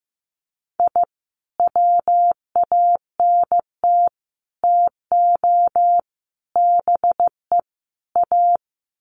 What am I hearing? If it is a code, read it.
Morse code, heard as a keyed tone, about 15 wpm, 700 Hz. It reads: I WANT TO BE A
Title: Sets of 5 Words Encore 15wpm